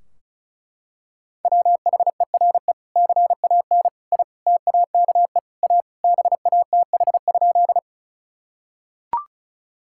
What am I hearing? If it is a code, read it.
Morse code, heard as a keyed tone, about 36 wpm, 700 Hz. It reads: WHERE CAN I TAKE A BATH?
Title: Sentences from Top 1600 Words 35wpm